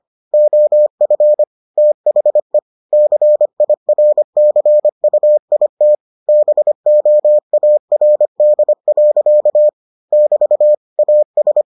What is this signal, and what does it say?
Morse code, heard as a keyed tone, about 25 wpm, 600 Hz. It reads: OF THE CIRCUIT BOARD. <BT> AS